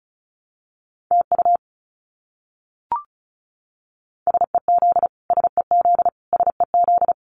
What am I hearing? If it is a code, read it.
Morse code, heard as a keyed tone, about 34 wpm, 700 Hz. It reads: TU SEZ SEZ SEZ
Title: Abbreviations Repeated 3x - 35wpm